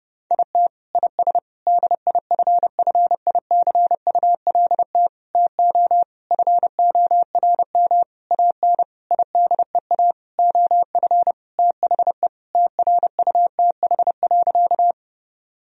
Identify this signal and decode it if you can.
Morse code, heard as a keyed tone, about 30 wpm, 700 Hz. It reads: IT IS DIFFICULT TO FORM AN IDEA OF THE TRUTH.
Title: Sentences from Top 1000 Words - Review Part 1 - 30wpm